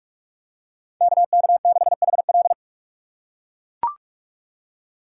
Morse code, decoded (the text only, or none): KK6HL